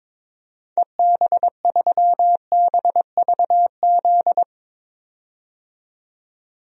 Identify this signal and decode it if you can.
Morse code, heard as a keyed tone, about 22 wpm, 700 Hz. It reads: EB3BVZ
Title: DXCC-100 Call Signs - 6 Characters - 22wpm